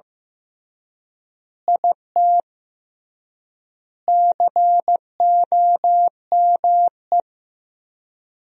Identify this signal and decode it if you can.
Morse code, heard as a keyed tone, about 15 wpm, 700 Hz. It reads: IT COME
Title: 3x Word-Spacing - Sentences from Top 100 Words - 15wpm